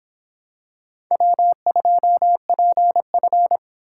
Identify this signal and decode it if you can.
Morse code, heard as a keyed tone, about 26 wpm, 700 Hz. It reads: W2PF